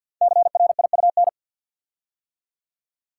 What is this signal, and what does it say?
Morse code, heard as a keyed tone, about 50 wpm, 700 Hz. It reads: CRIUN
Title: Code Groups - 5 Letters - 50wpm